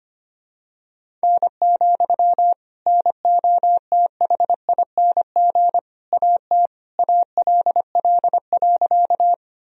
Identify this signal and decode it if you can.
Morse code, heard as a keyed tone, about 25 wpm, 700 Hz. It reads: N, NOTHING AT ALL.
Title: Sentences from Top 500 Words and Abbreviations 25wpm